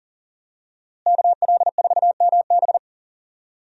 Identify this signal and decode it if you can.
Morse code, heard as a keyed tone, about 40 wpm, 700 Hz. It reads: KL4MB
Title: USA Call Signs - 5 Characters - Repeated 2x - 40wpm